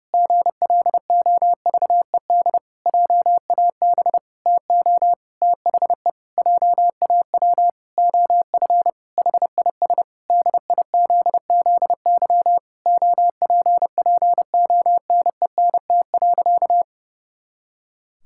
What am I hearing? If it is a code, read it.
Morse code, heard as a keyed tone, about 30 wpm, 700 Hz. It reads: GLOVED JAB TO THE JAW OF HIS DIZZY OPPONENT.
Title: Pangram Challenge v2 - 30wpm